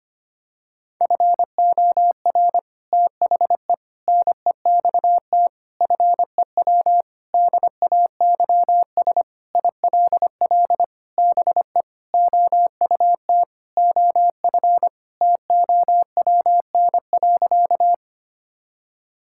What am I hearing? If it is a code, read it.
Morse code, heard as a keyed tone, about 25 wpm, 700 Hz. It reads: FOR THE NEXT FEW DAYS ILL BE OUT OF TOWN.
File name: practice / 25wpm / Sentences with 3-Word Phrases - 25wpm